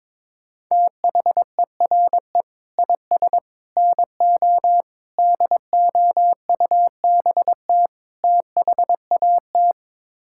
Morse code, heard as a keyed tone, about 22 wpm, 700 Hz. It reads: THERE IS NO DOUBT THAT
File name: practice / 22wpm / Build A Phrase - 5-Word Phrases - 22wpm